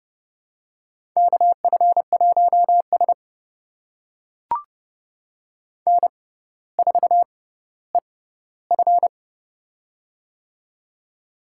Morse code, heard as a keyed tone, about 30 wpm, spaced as for 15 wpm, 700 Hz. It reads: KF1S N4EF
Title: USA Call Signs - 4 Characters - 6x Character Spacing - 30wpm